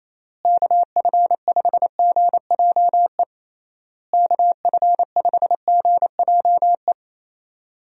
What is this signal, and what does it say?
Morse code, heard as a keyed tone, about 28 wpm, 700 Hz. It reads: KF5GJE KF5GJE
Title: USA Call Signs - 6 Characters - Repeated 3x - 28wpm